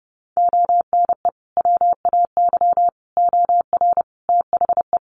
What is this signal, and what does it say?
Morse code, heard as a keyed tone, about 30 wpm, 700 Hz. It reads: ONE WAY OR THE
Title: Sets of 5 Words 30wpm